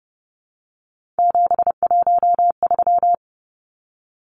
Morse code, heard as a keyed tone, about 30 wpm, 700 Hz. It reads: 713
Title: Numbers - 3-Digit 30wpm